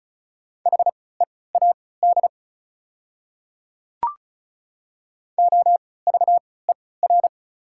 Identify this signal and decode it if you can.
Morse code, heard as a keyed tone, about 36 wpm, spaced as for 20 wpm, 690 Hz. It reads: HEAD OVER
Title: 3x Character Spacing - Top 500 Words - 35wpm